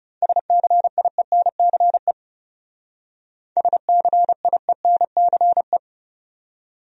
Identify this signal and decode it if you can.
Morse code, heard as a keyed tone, about 39 wpm, 700 Hz. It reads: SCIENCE SCIENCE